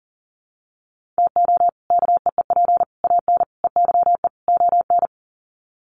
Morse code, heard as a keyed tone, about 40 wpm, 700 Hz. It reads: TO KEEP AN EYE ON
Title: Sets of 5 Words Encore 40wpm